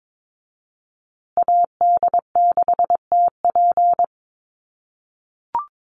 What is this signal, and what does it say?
Morse code, heard as a keyed tone, about 22 wpm, 700 Hz. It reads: AD6TP
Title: USA Call Signs Repeated 3x 22wpm